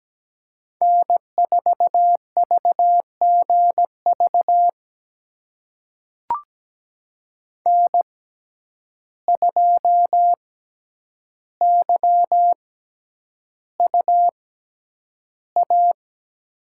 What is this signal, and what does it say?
Morse code, heard as a keyed tone, about 17 wpm, spaced as for 9 wpm, 700 Hz. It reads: N4VGV N2YUA
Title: USA Call Signs - 5 Characters - 6x Character Spacing - 17wpm